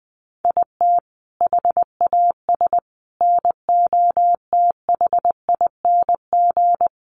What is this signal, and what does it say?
Morse code, heard as a keyed tone, about 20 wpm, 700 Hz. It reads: IT HAS NOTHING
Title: Sets of 5 Words Encore 20wpm